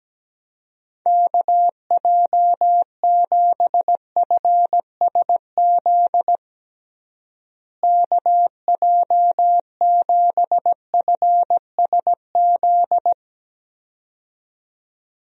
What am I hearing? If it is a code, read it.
Morse code, heard as a keyed tone, about 17 wpm, 700 Hz. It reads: KJ7FSZ KJ7FSZ
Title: USA Call Signs - 6 Characters - Repeated 2x - 17wpm